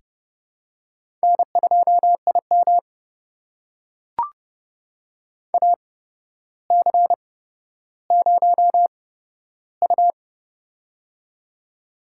Morse code, heard as a keyed tone, about 30 wpm, spaced as for 13 wpm, 700 Hz. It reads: N2IM AC0U
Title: USA Call Signs - 8x Character Spacing - 30wpm